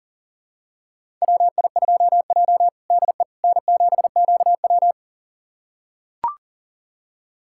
Morse code, heard as a keyed tone, about 40 wpm, 700 Hz. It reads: WI2J DE N7QW
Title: USA Callsign DE Callsign - 4 Characters - 40wpm